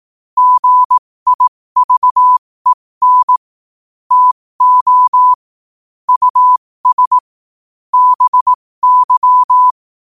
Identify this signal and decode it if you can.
Morse code, heard as a keyed tone, about 18 wpm, spaced as for 15 wpm, 1000 Hz. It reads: GIVEN TO US BY
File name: qotdfiles / 15wpm